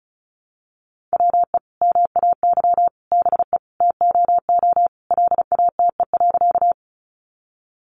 Morse code, heard as a keyed tone, about 34 wpm, 700 Hz. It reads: WE MAY BE TOO LATE.
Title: Sentences from Top 300 Words 35wpm